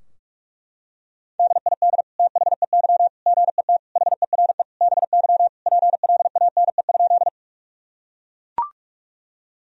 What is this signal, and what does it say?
Morse code, heard as a keyed tone, about 42 wpm, 700 Hz. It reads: DID THEY GET HERE BY PLANE?
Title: Sentences from Top 1000 Words - Review Part 3 - 45wpm